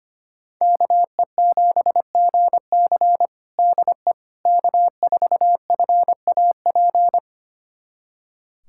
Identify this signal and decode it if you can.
Morse code, heard as a keyed tone, about 25 wpm, 700 Hz. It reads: KE7GC DE K4FAP
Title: USA Callsign DE Callsign 25wpm